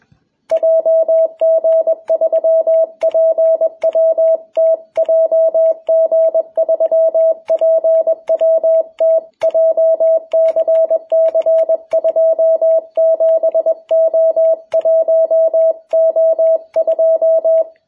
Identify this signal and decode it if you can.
Morse code, heard as a keyed tone, about 21 wpm, 660 Hz. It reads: JG3PWT JG3PWT JCC27O1O2
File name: jh8cbh / 2021cw